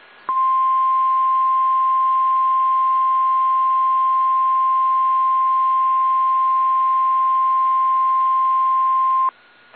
Emergency Alert System (SAME) tones (the 1050 Hz attention tone). It is heard from 0.2 to 9.4 s.